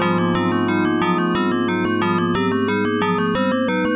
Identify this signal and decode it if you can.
There is no code.